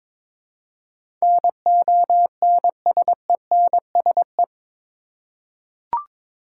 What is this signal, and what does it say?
Morse code, heard as a keyed tone, about 22 wpm, 700 Hz. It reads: NONSENSE
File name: practice / 22wpm / Eight Letter Words - 22wpm